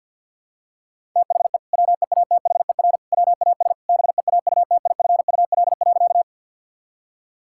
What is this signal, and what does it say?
Morse code, heard as a keyed tone, about 50 wpm, 700 Hz. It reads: THE WEATHER WAS BEAUTIFUL.
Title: Sentences from Top 1000 Words - Review Part 5 - 50wpm